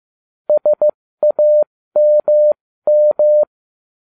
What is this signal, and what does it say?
Morse code, heard as a keyed tone, about 15 wpm, spaced as for 13 wpm, 600 Hz. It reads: SAMM